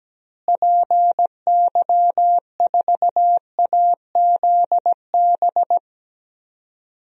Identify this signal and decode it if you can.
Morse code, heard as a keyed tone, about 17 wpm, 700 Hz. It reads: PY4AZB
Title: DXCC-100 Call Signs - 6 Characters - Repeated 3x - 17wpm